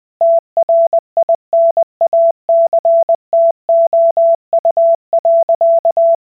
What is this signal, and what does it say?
Morse code, heard as a keyed tone, about 20 wpm, 670 Hz. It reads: TRINACTOU.